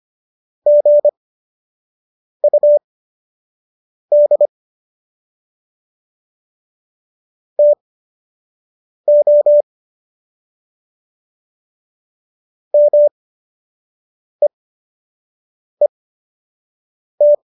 Morse code, heard as a keyed tone, about 25 wpm, spaced as for 6 wpm, 590 Hz. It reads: GUD TO MEET
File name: qso012 06